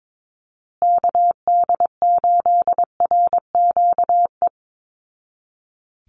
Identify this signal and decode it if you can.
Morse code, heard as a keyed tone, about 22 wpm, 700 Hz. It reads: KD8RQE
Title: Call Signs Repeated 2x - 22wpm